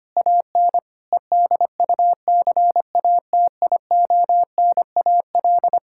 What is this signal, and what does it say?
Morse code, heard as a keyed tone, about 25 wpm, 700 Hz. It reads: AN EDUCATIONAL
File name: practice / 25wpm / Difficult Sentences with Hints - 25wpm